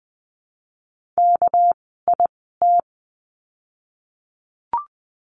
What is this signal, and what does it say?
Morse code, heard as a keyed tone, about 20 wpm, spaced as for 15 wpm, 700 Hz. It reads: KIT
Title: 199 Lesson 32f - Letter k - Words 20wpm